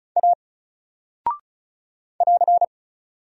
Morse code, heard as a keyed tone, about 36 wpm, 700 Hz. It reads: A <AR>
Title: Single Character (Letter:Number:Punctuation:Prosign) Alt 35wpm